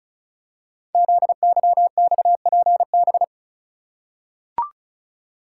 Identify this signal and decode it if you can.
Morse code, heard as a keyed tone, about 36 wpm, 700 Hz. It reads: ZYXPB
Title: Code Groups - 5 Letter:Numbers - 35wpm